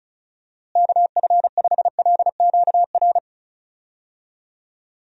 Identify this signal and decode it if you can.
Morse code, heard as a keyed tone, about 34 wpm, 700 Hz. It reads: KF5LQR
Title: USA Call Signs - 6 Characters - Repeated 3x - 35wpm